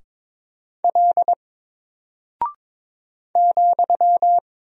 Morse code, heard as a keyed tone, about 22 wpm, 700 Hz. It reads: L ,
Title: Single Character (Letter:Number:Punctuation:Prosign) 22wpm